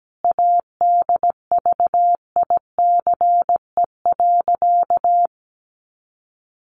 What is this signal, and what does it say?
Morse code, heard as a keyed tone, about 17 wpm, 700 Hz. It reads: ADVICE.
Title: Sentences from Top 1100 Words 17wpm